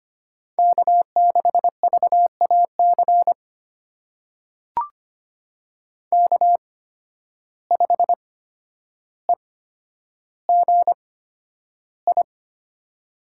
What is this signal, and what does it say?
Morse code, heard as a keyed tone, about 25 wpm, spaced as for 11 wpm, 700 Hz. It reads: K6VAC K5EGI
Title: USA Call Signs - 5 Characters - 8x Character Spacing - 25wpm